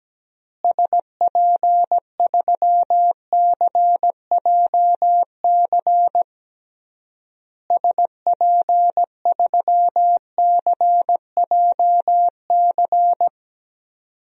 Morse code, heard as a keyed tone, about 17 wpm, 700 Hz. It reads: SP3CJC SP3CJC
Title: DXCC-100 Call Signs - 6 Characters - Repeated 3x - 17wpm